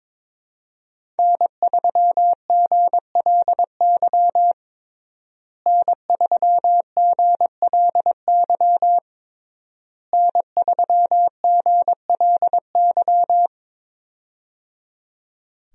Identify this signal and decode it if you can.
Morse code, heard as a keyed tone, about 22 wpm, 700 Hz. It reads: N3GLY N3GLY N3GLY
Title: USA Call Signs - 5 Characters - Repeated 3x - 22wpm